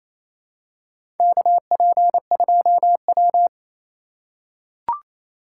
Morse code, heard as a keyed tone, about 28 wpm, 700 Hz. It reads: KP2W
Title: USA Call Signs - 4 Characters - 3x Character Spacing - 28wpm